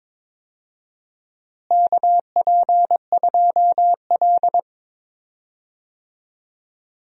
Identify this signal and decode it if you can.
Morse code, heard as a keyed tone, about 22 wpm, 700 Hz. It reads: KP2L
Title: DXCC-100 Call Signs - 4 Characters - Repeated 2x - 22wpm